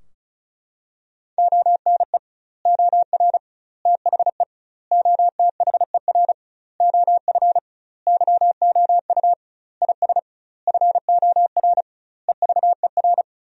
Morse code, heard as a keyed tone, about 36 wpm, 700 Hz. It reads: ONE OR THE OTHER OF YOU IS FOR EVER
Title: 2x Word-Spacing - Sentences from Top 400 Words - 35wpm